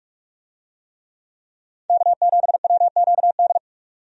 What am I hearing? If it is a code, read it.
Morse code, heard as a keyed tone, about 46 wpm, 700 Hz. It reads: K7WQD
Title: USA Call Signs - 5 Characters - 45wpm